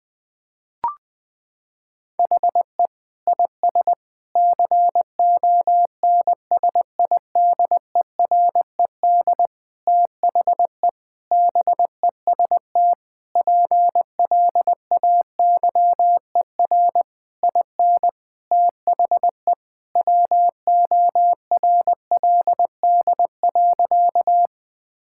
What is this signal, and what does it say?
Morse code, heard as a keyed tone, about 20 wpm, 700 Hz. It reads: HE IS CONSIDERED THE BEST PLAYER IN THE WORLD.
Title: Sentences with 6-Word Phrases - 20wpm